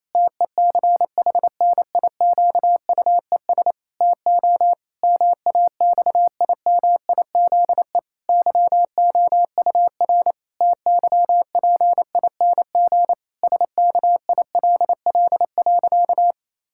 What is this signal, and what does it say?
Morse code, heard as a keyed tone, about 28 wpm, 700 Hz. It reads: TECHNIQUES TO MAXIMIZE YOUR TYPING SKILL.
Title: Pangram Challenge v2 - 28wpm